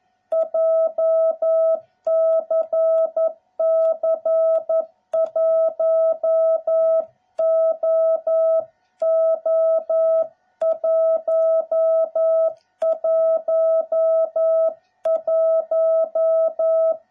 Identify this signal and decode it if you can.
Morse code, heard as a keyed tone, about 11 wpm, 660 Hz. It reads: JCC 1OO111